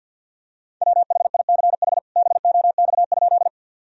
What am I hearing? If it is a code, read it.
Morse code, heard as a keyed tone, about 50 wpm, 700 Hz. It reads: WHICH BOX?
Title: Sentences from Top 1000 Words - Review Part 2 - 50wpm